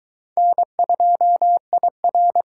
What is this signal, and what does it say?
Morse code, heard as a keyed tone, about 24 wpm, 700 Hz. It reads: N2IR